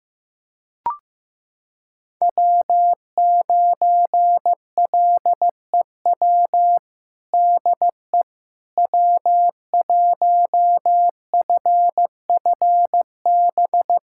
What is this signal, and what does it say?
Morse code, heard as a keyed tone, about 15 wpm, 700 Hz. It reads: W9LEW DE W1FFB
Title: USA Callsign DE Callsign - 5 Characters - 15wpm